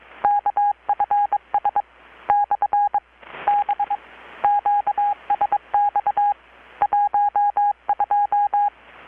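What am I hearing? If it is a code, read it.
Morse code, heard as a keyed tone, about 22 wpm, spaced as for 18 wpm, 820 Hz. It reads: KFS/B QSX 12
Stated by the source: Morse from KPH's Night of Nights XX maritime radiotelegraphy event, received in Chiapas, Mexico about 3,800 km away on a Kenwood R-600 with a grounded 10 m long-wire antenna.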